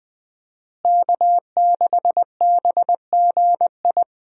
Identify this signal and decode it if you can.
Morse code, heard as a keyed tone, about 20 wpm, 700 Hz. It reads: K6BGI
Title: USA Call Signs - 5 Characters - Repeated 2x - 20wpm